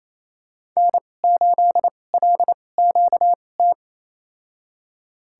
Morse code, heard as a keyed tone, about 28 wpm, spaced as for 20 wpm, 700 Hz. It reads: N8LQT
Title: USA Call Signs - 5 Characters - 2x Character Spacing - 28wpm